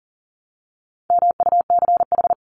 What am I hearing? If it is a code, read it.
Morse code, heard as a keyed tone, about 38 wpm, 700 Hz. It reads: MUCH
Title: Top 200 Words 40wpm